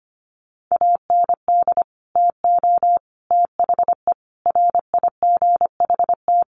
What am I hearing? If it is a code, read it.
Morse code, heard as a keyed tone, about 25 wpm, 700 Hz. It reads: AND TO THE RIGHT
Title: Sentences from Top 1300 Words 25wpm